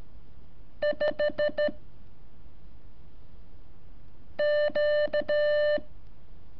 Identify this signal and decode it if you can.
Morse code, heard as a keyed tone, about 13 wpm, 590 Hz. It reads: 5 Q
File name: curso / ejercicio 20